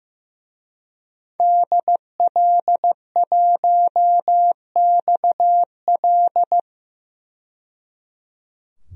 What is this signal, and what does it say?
Morse code, heard as a keyed tone, about 15 wpm, 700 Hz. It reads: DL1XL
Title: DXCC-100 Call Signs - 5 Characters - 15wpm